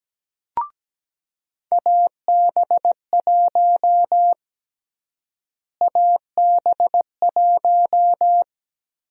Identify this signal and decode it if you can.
Morse code, heard as a keyed tone, about 17 wpm, 700 Hz. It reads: AB1 AB1
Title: USA Call Sign Prefixes - Repeated 2x - 17wpm